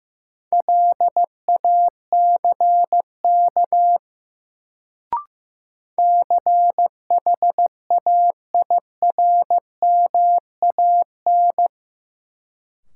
Morse code, heard as a keyed tone, about 15 wpm, 700 Hz. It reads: LACK CHAIRMAN
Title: Sets of 3 Words - New Word Drill - 15wpm